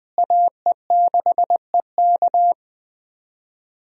Morse code, heard as a keyed tone, about 20 wpm, 700 Hz. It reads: AE6EK
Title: USA Call Signs - 5 Characters - Repeated 3x - 20wpm